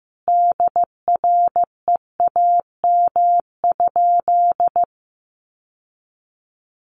Morse code, heard as a keyed tone, about 15 wpm, 700 Hz. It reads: DREAM?
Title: Sentences from Top 1500 Words 15wpm